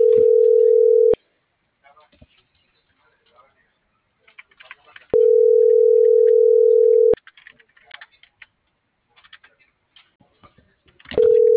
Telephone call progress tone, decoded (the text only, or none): ringback tone